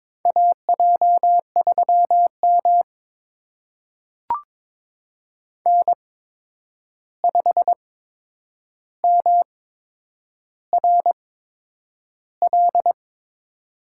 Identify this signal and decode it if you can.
Morse code, heard as a keyed tone, about 22 wpm, spaced as for 9 wpm, 700 Hz. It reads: AJ3M N5MRL
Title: USA Call Signs - 8x Character Spacing - 22wpm